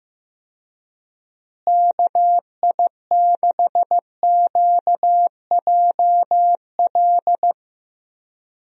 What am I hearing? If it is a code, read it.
Morse code, heard as a keyed tone, about 16 wpm, 700 Hz. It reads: KI6QJL
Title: USA Call Signs - 6 Characters - Repeated 2x - 15wpm